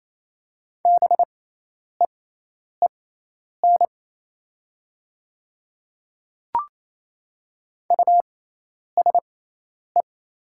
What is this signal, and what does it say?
Morse code, heard as a keyed tone, about 28 wpm, spaced as for 10 wpm, 700 Hz. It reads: BEEN USE
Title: 6x Character Spacing - Top 100 Words - 28wpm